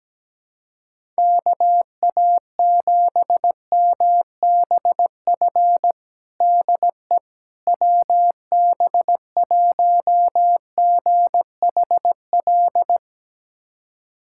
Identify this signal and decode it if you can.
Morse code, heard as a keyed tone, about 17 wpm, 700 Hz. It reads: KA7MBF DE WB1GHL
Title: USA Callsign DE Callsign - 6 Characters - 17wpm